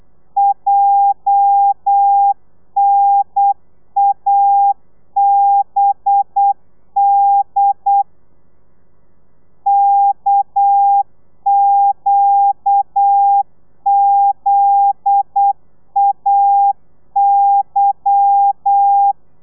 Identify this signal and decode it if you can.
Morse code, heard as a keyed tone, about 8 wpm, 800 Hz. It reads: JNABD KQZAY
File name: Lez11